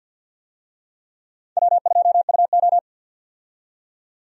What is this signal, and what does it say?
Morse code, heard as a keyed tone, about 50 wpm, 700 Hz. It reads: W2UO
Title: USA Call Signs - 4 Characters - 50wpm